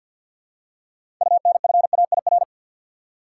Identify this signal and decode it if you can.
Morse code, heard as a keyed tone, about 52 wpm, 700 Hz. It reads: UNFAIR
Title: Six Letter Words - 50wpm